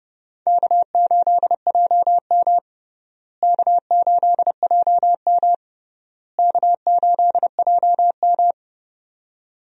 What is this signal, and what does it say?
Morse code, heard as a keyed tone, about 30 wpm, 700 Hz. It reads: K8JM K8JM K8JM
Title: USA Call Signs - 4 Characters - Repeated 3x - 30wpm